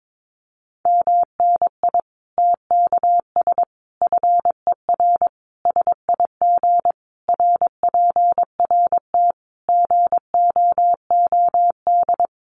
Morse code, heard as a keyed tone, about 22 wpm, 700 Hz. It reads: MNI TKS FER SIG RPRT GOOD
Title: Common QSO Phrases 22wpm